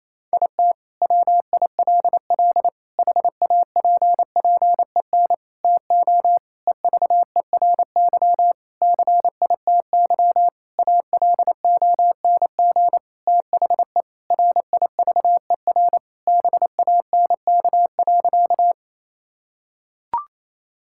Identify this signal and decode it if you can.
Morse code, heard as a keyed tone, about 28 wpm, 700 Hz. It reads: IT WILL HAPPEN TO EVERY CITY ALONG THE RIVER BANK.
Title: Sentences from Top 1000 Words - Review Part 6 - 28wpm